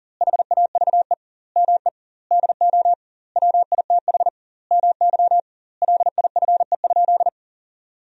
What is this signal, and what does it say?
Morse code, heard as a keyed tone, about 37 wpm, 700 Hz. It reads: HAVE ME DO WITH MY LIFE?
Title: 2x Word-Spacing - Sentences from Top 300 Words - 40wpm